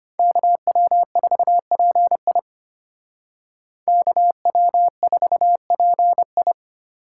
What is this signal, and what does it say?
Morse code, heard as a keyed tone, about 32 wpm, 700 Hz. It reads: KW4PI KW4PI